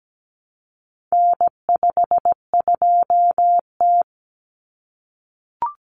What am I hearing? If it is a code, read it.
Morse code, heard as a keyed tone, about 17 wpm, 700 Hz. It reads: N52T
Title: Common Cut Numbers - 4-Digit 17wpm